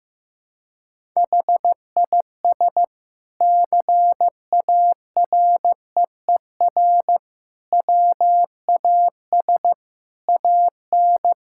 Morse code, heard as a keyed tone, about 15 wpm, 700 Hz. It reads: HIS CAREER WAS AN
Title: Sentences from Top 1000 Words - Review Part 6 - 15wpm